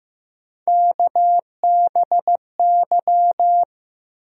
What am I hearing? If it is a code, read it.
Morse code, heard as a keyed tone, about 15 wpm, 700 Hz. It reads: KBY
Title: USA Call Sign Postfixes - 3 Letters 15wpm